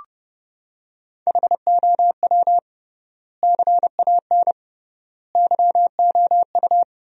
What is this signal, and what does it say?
Morse code, heard as a keyed tone, about 32 wpm, 700 Hz. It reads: HOW CAN YOU
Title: 3x Word-Spacing - Sentences from Top 100 Words - 30wpm